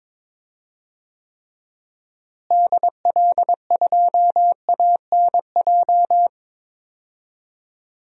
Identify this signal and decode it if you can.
Morse code, heard as a keyed tone, about 22 wpm, 700 Hz. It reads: DL2ANJ
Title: DXCC-100 Call Signs - 6 Characters - Repeated 3x - 22wpm